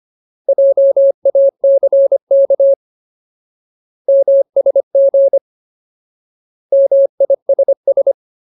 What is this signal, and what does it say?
Morse code, heard as a keyed tone, about 25 wpm, 550 Hz. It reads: JACK MSG MISS
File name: Ultimate Word List / 25WPM Ultimate